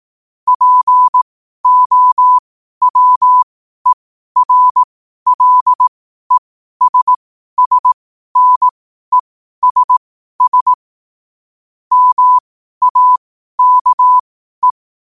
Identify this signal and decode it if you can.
Morse code, heard as a keyed tone, about 18 wpm, spaced as for 12 wpm, 1000 Hz. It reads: POWERLESSNESS MAKE
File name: qotdfiles / 12wpm